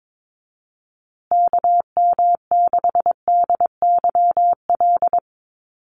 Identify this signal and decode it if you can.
Morse code, heard as a keyed tone, about 22 wpm, 700 Hz. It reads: KM6DYL
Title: USA Call Signs Repeated 3x 22wpm